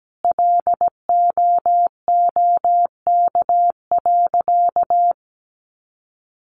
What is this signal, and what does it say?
Morse code, heard as a keyed tone, about 17 wpm, 700 Hz. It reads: LOOK.
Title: A Sound Of Thunder 17wpm